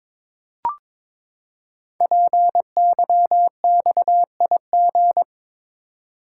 Morse code, heard as a keyed tone, about 22 wpm, 700 Hz. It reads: PYXIG
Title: Code Groups - 5 Letters - 22wpm